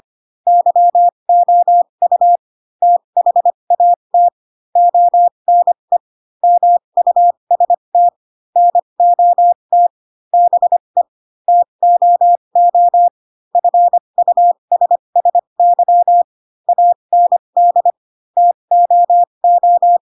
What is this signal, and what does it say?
Morse code, heard as a keyed tone, about 25 wpm, spaced as for 22 wpm, 700 Hz. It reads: YOU THAT ONE MUST NOT BE TOO FUSSY AND TOO
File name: Pinocchio / Pinocchio ch7 18 25 ik1hge